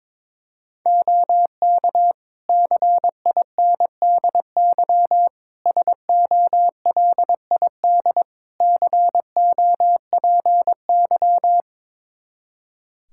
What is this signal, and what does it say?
Morse code, heard as a keyed tone, about 21 wpm, 700 Hz. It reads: OK CINDY SOLID COPY
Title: K1USN SST QSO Practice 22wpm